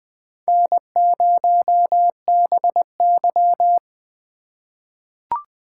Morse code, heard as a keyed tone, about 20 wpm, 700 Hz. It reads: N0BY
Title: USA Call Signs - 4 Characters - 3x Character Spacing - 20wpm